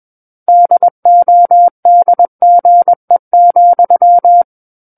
Morse code, heard as a keyed tone, about 21 wpm, 700 Hz. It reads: DODGE,
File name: Book One / 12-Chapter 12 - 21 WPM